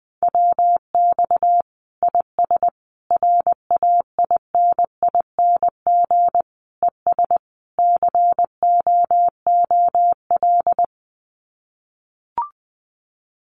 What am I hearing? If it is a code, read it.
Morse code, heard as a keyed tone, about 20 wpm, 700 Hz. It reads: WX IS RAINING ES COOL
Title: Common QSO Phrases 20wpm